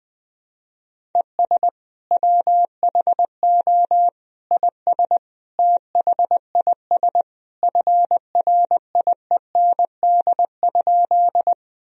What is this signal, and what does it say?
Morse code, heard as a keyed tone, about 20 wpm, 700 Hz. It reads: ES WHO IS THIS FRIEND?
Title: Sentences from Top 300 Words and Abbreviations 20wpm